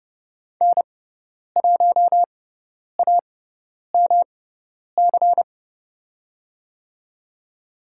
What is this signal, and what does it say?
Morse code, heard as a keyed tone, about 30 wpm, spaced as for 10 wpm, 700 Hz. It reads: N1AMC
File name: Lesson 18g - Letter m - Call Signs 30wpm